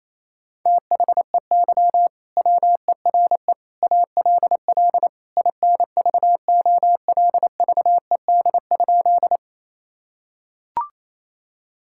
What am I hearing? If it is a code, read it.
Morse code, heard as a keyed tone, about 28 wpm, 700 Hz. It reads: THEY WERE ALL INVOLVED?
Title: Sentences from Top 1000 Words - Review Part 5 - 28wpm